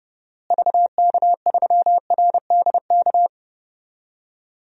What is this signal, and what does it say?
Morse code, heard as a keyed tone, about 29 wpm, 700 Hz. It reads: VK3RDK
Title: DXCC-100 Call Signs - 6 Characters - Repeated 2x - 30wpm